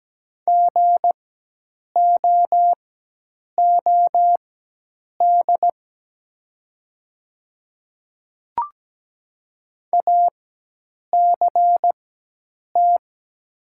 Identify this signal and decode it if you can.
Morse code, heard as a keyed tone, about 17 wpm, spaced as for 8 wpm, 700 Hz. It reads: GOOD ACT
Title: 4x Character Spacing - Top 200 Words - 17wpm